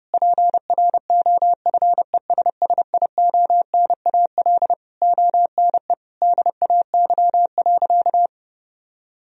Morse code, heard as a keyed tone, about 30 wpm, 700 Hz. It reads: PROFESSIONAL ONE DAY.
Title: Sentences from Top 1000 Words - Review Part 4 - 30wpm